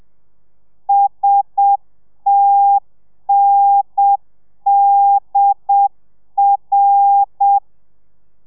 Morse code, heard as a keyed tone, about 7 wpm, 800 Hz. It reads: STNDR